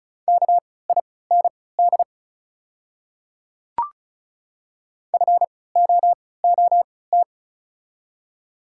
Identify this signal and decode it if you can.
Morse code, heard as a keyed tone, about 34 wpm, spaced as for 20 wpm, 690 Hz. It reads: KIND FOOT
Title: 3x Character Spacing - Top 500 Words - 35wpm